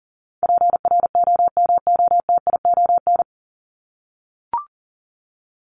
Morse code, heard as a keyed tone, about 40 wpm, 700 Hz. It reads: PROMOTION
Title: Top 1501-1750 Words 40wpm